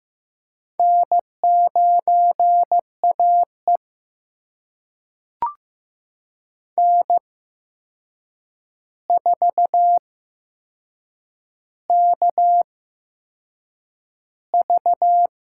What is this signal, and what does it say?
Morse code, heard as a keyed tone, about 15 wpm, spaced as for 6 wpm, 700 Hz. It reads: N9AE N4KV
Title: USA Call Signs - 4 Characters - 8x Character Spacing - 15wpm